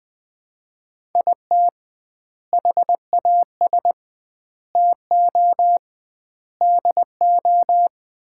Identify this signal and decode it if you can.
Morse code, heard as a keyed tone, about 21 wpm, 700 Hz. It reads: IT HAS TO DO
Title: 2x Word-Spacing - Sets of 5 Words - 20wpm